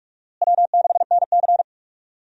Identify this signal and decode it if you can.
Morse code, heard as a keyed tone, about 46 wpm, 700 Hz. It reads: W6NC
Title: USA Call Signs - 4 Characters - Repeated 3x - 45wpm